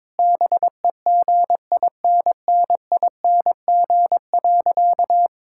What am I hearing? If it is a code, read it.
Morse code, heard as a keyed tone, about 22 wpm, 700 Hz. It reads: BEGINNING.
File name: Difficult Sentences with Hints - 22wpm